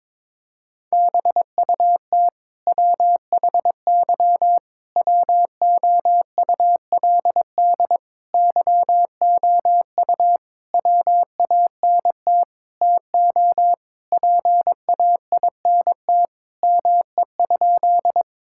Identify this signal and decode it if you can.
Morse code, heard as a keyed tone, about 22 wpm, 700 Hz. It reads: BUT WHY WOULD YOU WANT TO PAINT ME?
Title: Sentences from Top 1000 Words - Review Part 6 - 22wpm